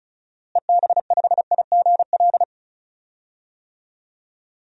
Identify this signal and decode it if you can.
Morse code, heard as a keyed tone, about 36 wpm, 700 Hz. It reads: EB5IGL
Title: DXCC-100 Call Signs - 6 Characters - Repeated 3x - 35wpm